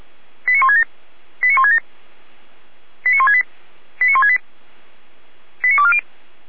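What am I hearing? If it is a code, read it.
Five-tone selective call (ZVEI), heard as quick runs of five tones, tones of about 70 ms. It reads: 89167 89167 89168 89168 89240
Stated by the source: ZVEI five-tone selective call received by SDR, fire brigade Feuerwehr Orlamünde.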